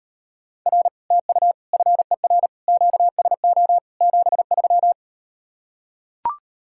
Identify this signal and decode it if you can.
Morse code, heard as a keyed tone, about 38 wpm, 700 Hz. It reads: R TU FER QSO 73